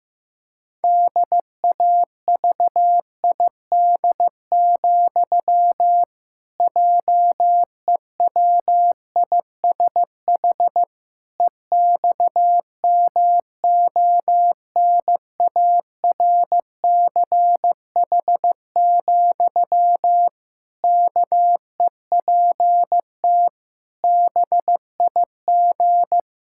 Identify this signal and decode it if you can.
Morse code, heard as a keyed tone, about 15 wpm, 700 Hz. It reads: DAVID, JEWISH EXMONARCH, KEPT BIG
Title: Pangram Challenge v2 - 15wpm